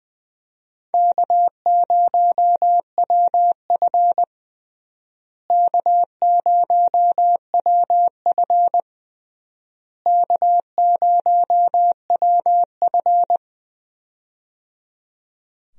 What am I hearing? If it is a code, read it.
Morse code, heard as a keyed tone, about 21 wpm, 700 Hz. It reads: K0WF K0WF K0WF
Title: USA Call Signs - 4 Characters - Repeated 3x - 20wpm